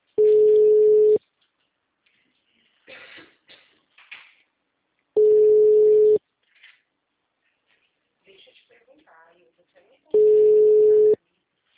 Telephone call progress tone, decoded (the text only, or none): ringback tone